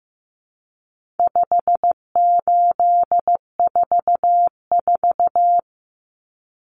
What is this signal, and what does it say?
Morse code, heard as a keyed tone, about 15 wpm, 700 Hz. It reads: 5844
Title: Common Cut Numbers - 4-Digit 15wpm